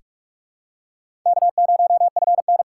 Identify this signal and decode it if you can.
Morse code, heard as a keyed tone, about 44 wpm, 700 Hz. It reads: K0FN
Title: USA Call Signs - 4x Character Spacing - 45wpm